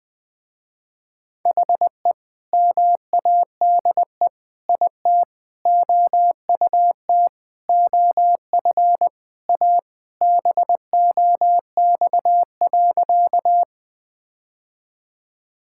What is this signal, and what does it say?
Morse code, heard as a keyed tone, about 20 wpm, 700 Hz. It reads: HE MADE IT OUT OF A BOX.
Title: Sentences with 3-Word Phrases - Spoken Beginning - 20wpm